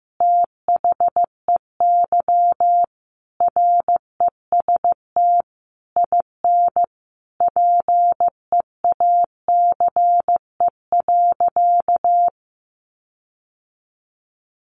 Morse code, heard as a keyed tone, about 15 wpm, 700 Hz. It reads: THEY REST IN PEACE.
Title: Sentences from Top 1500 Words 15wpm